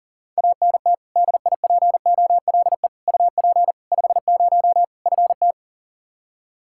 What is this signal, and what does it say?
Morse code, heard as a keyed tone, about 40 wpm, 700 Hz. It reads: ANT DIPOLE UP 50 FT
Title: Practice QSO - Exchange 2 - Info - 40wpm